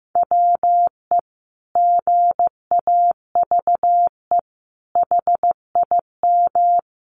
Morse code, heard as a keyed tone, about 15 wpm, 700 Hz. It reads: WE GAVE HIM
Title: Sentences from Top 600 Words 15wpm